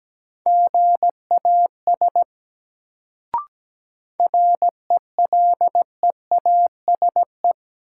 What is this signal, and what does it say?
Morse code, heard as a keyed tone, about 17 wpm, 700 Hz. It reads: GAS RELEASE
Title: Sentences from Top 1100 Words - New Word Drill Alt - 17wpm